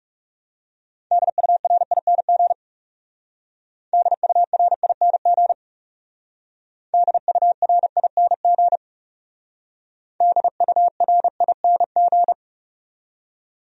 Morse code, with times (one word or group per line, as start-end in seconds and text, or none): 1.1-2.5 DURING
3.9-5.5 DURING
6.9-8.8 DURING
10.2-12.3 DURING